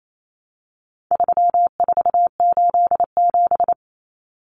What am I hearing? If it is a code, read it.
Morse code, heard as a keyed tone, about 28 wpm, 700 Hz. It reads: 3487
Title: Numbers - 4-Digit 28wpm